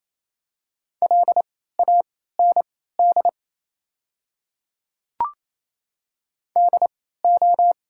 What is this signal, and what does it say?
Morse code, heard as a keyed tone, about 27 wpm, spaced as for 16 wpm, 700 Hz. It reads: LAND DO
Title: 3x Character Spacing - Top 200 Words - 28wpm